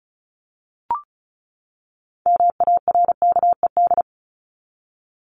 Morse code, heard as a keyed tone, about 34 wpm, 700 Hz. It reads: MARKED
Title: ED Suffix - Top 100 Words with Base Words 35wpm